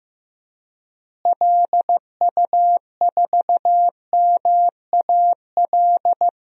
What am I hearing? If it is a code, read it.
Morse code, heard as a keyed tone, about 15 wpm, 700 Hz. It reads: LU4MAL
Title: DXCC-100 Call Signs - 6 Characters - Repeated 3x - 15wpm